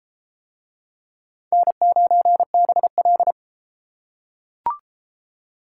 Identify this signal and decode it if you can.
Morse code, heard as a keyed tone, about 29 wpm, 700 Hz. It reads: N9BL